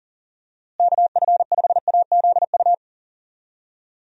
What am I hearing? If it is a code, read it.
Morse code, heard as a keyed tone, about 37 wpm, 700 Hz. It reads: KF5AZU
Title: USA Call Signs - 6 Characters - Repeated 2x - 40wpm